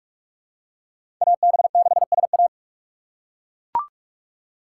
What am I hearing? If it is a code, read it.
Morse code, heard as a keyed tone, about 41 wpm, 700 Hz. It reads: AB6SA
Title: USA Call Signs - 5 Characters - 3x Character Spacing - 45wpm